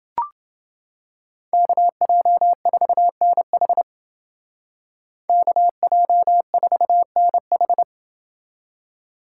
Morse code, heard as a keyed tone, about 29 wpm, 700 Hz. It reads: KJ4NH KJ4NH